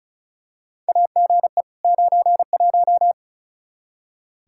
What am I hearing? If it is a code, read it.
Morse code, heard as a keyed tone, about 36 wpm, 700 Hz. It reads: AGE 91
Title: Practice QSO - Exchange 3 - More Info - 35wpm